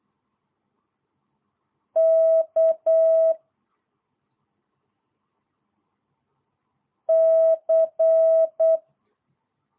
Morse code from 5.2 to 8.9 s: C